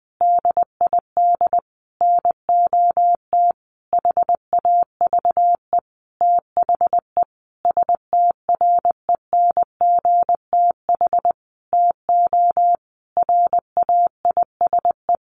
Morse code, heard as a keyed tone, about 20 wpm, 700 Hz. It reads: DID NOT HAVE THE STRENGTH TO RAISE
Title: Sentences from Top 1400 Words 20wpm